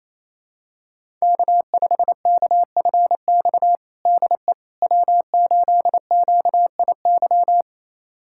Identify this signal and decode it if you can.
Morse code, heard as a keyed tone, about 27 wpm, 700 Hz. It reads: K5KFX DE W8QIY
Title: USA Callsign DE Callsign - 5 Characters - 28wpm